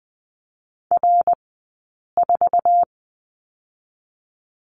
Morse code, heard as a keyed tone, about 20 wpm, 700 Hz. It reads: R 4
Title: 052 Lesson 10d - Letter r - Two Characters 20wpm